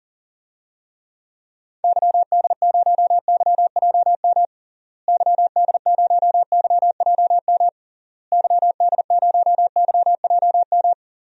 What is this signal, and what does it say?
Morse code, heard as a keyed tone, about 40 wpm, 700 Hz. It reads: YD0YJM YD0YJM YD0YJM
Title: DXCC-100 Call Signs - 6 Characters - Repeated 3x - 40wpm